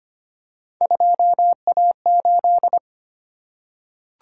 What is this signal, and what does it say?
Morse code, heard as a keyed tone, about 25 wpm, 700 Hz. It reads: 2A8
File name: Three Letter-Number Combinations - 25wpm